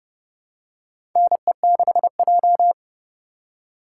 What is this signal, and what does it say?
Morse code, heard as a keyed tone, about 30 wpm, 700 Hz. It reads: NE6J
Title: USA Call Signs - 4 Characters - 2x Character Spacing - 30wpm